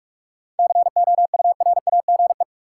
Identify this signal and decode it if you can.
Morse code, heard as a keyed tone, about 46 wpm, 700 Hz. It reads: COURAGE
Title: Seven Letter Words - 45wpm